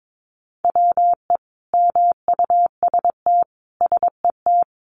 Morse code, heard as a keyed tone, about 22 wpm, 700 Hz. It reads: WE MUST SET
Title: Sentences from Top 1400 Words 22wpm